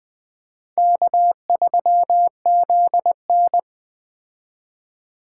Morse code, heard as a keyed tone, about 21 wpm, 700 Hz. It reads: K3ZN
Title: Speed Racing - USA 1x2 Call Signs 17wpm